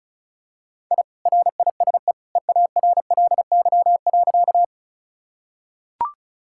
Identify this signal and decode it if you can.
Morse code, heard as a keyed tone, about 36 wpm, 700 Hz. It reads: I RISE EARLY.
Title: Sentences from Top 1000 Words - Review Part 6 - 35wpm